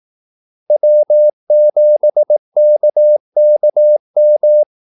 Morse code, heard as a keyed tone, about 18 wpm, 600 Hz. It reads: W7KKM
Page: Kaldesignaler 18 wpm